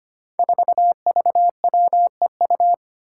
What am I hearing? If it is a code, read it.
Morse code, heard as a keyed tone, about 25 wpm, 700 Hz. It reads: 4VWEU
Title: Five Challenging Letter-Number Combinations 25wpm